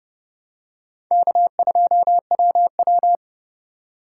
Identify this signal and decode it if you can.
Morse code, heard as a keyed tone, about 30 wpm, 700 Hz. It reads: K2WW
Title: USA Call Signs - 4 Characters - 3x Character Spacing - 30wpm